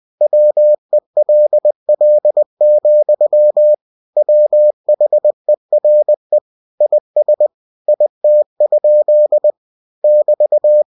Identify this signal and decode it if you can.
Morse code, heard as a keyed tone, about 20 wpm, 600 Hz. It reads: WELL, WHERE IS IT? <BT>
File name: cw-books / winnie-the-pooh / winnie-20wpm-page-0099